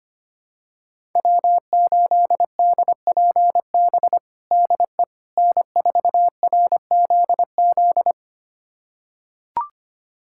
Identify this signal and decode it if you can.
Morse code, heard as a keyed tone, about 25 wpm, 700 Hz. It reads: W8DPB DE N4RZZ
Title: USA Callsign DE Callsign - 5 Characters - 25wpm